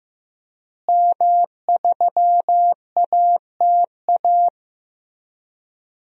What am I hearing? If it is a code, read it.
Morse code, heard as a keyed tone, about 15 wpm, 700 Hz. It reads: M3ATA
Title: DXCC-100 Call Signs - 5 Characters - Repeated 3x - 15wpm